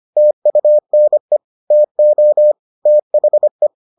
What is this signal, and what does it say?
Morse code, heard as a keyed tone, about 25 wpm, 600 Hz. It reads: TUNE TO THE